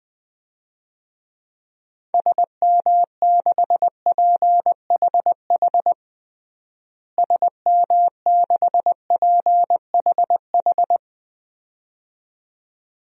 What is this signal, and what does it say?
Morse code, heard as a keyed tone, about 20 wpm, 700 Hz. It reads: SM6PHH SM6PHH
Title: DXCC-100 Call Signs - 6 Characters - Repeated 2x - 20wpm